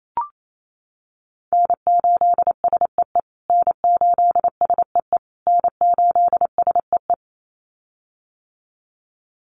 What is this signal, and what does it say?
Morse code, heard as a keyed tone, about 28 wpm, 700 Hz. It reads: N8SEE N8SEE N8SEE
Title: USA Call Signs Repeated 3x 28wpm